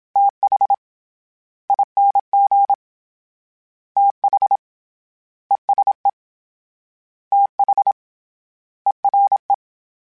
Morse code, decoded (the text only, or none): TH ING TH ESE TH ERE